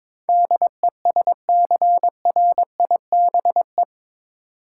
Morse code, heard as a keyed tone, about 21 wpm, 700 Hz. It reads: DESCRIBE
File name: practice / 22wpm / Eight Letter Words - 22wpm